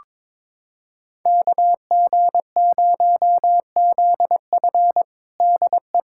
Morse code, heard as a keyed tone, about 22 wpm, 700 Hz. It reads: KG0ZF DE
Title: USA Callsign DE Callsign - 5 Characters - 22wpm